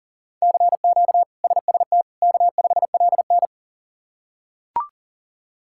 Morse code, heard as a keyed tone, about 40 wpm, 700 Hz. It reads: CQ SST K5LN
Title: K1USN SST QSO Practice - 40wpm